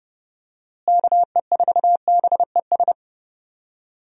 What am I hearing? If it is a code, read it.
Morse code, heard as a keyed tone, about 30 wpm, 700 Hz. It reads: KE4BES
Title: USA Call Signs - 6 Characters - Repeated 2x - 30wpm